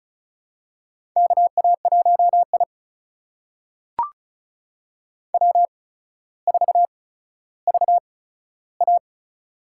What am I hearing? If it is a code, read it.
Morse code, heard as a keyed tone, about 36 wpm, spaced as for 15 wpm, 700 Hz. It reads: KA1I W4VA
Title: USA Call Signs - 4 Characters - 8x Character Spacing - 35wpm